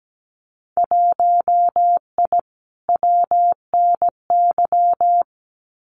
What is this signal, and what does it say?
Morse code, heard as a keyed tone, about 17 wpm, 700 Hz. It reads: 1I WNY
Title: Winter Field Day Exchanges Encore 17wpm